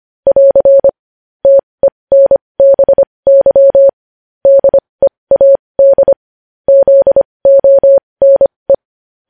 Morse code, heard as a keyed tone, about 25 wpm, spaced as for 20 wpm, 550 Hz. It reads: <AR> TENBY DEAD ZONE
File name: podcast / News headlines-1x-25x20